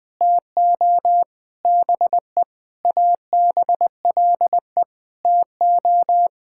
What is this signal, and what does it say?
Morse code, heard as a keyed tone, about 21 wpm, 700 Hz. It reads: TO BE ABLE TO
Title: Sentences with 6-Word Phrases - 20wpm